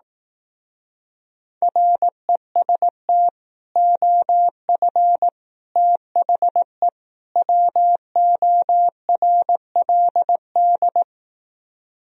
Morse code, heard as a keyed tone, about 18 wpm, 700 Hz. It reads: REST OF THE WORLD
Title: Speed-Racing - Sets of 4 Words 15wpm